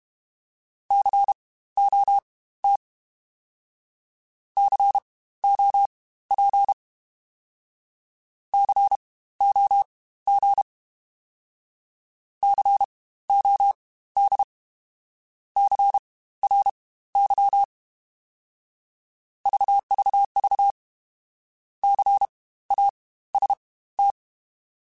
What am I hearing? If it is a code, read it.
Morse code, heard as a keyed tone, about 32 wpm, spaced as for 16 wpm, 800 Hz. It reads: COT COP COG COD CRY VVV CAST
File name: Spaced Alphas / Spaced Alphas mp3s / Cs sp @30